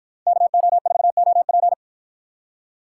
Morse code, heard as a keyed tone, about 53 wpm, 700 Hz. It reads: KO4OP